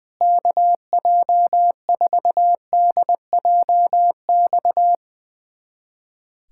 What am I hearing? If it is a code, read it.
Morse code, heard as a keyed tone, about 20 wpm, 700 Hz. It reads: KJ4DJX
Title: USA Callsign DE Callsign - 6 Characters - 20wpm